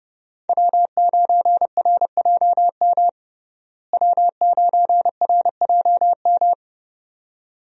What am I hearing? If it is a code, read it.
Morse code, heard as a keyed tone, about 32 wpm, 700 Hz. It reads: W9RJM W9RJM
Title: USA Call Signs - 5 Characters - Repeated 2x - 30wpm